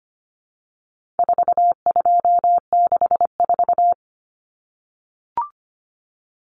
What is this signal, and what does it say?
Morse code, heard as a keyed tone, about 25 wpm, 700 Hz. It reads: 4264
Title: Numbers - 4-Digit 25wpm